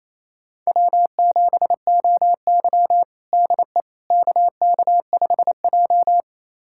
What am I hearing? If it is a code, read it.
Morse code, heard as a keyed tone, about 28 wpm, 700 Hz. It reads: W7OY DE KK5J
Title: USA Callsign DE Callsign - 4 Characters - 28wpm